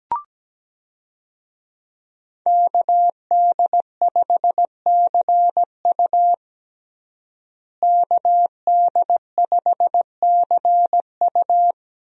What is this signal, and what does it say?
Morse code, heard as a keyed tone, about 17 wpm, 700 Hz. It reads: KD5CU KD5CU
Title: USA Call Signs - 5 Characters - Repeated 3x - 17wpm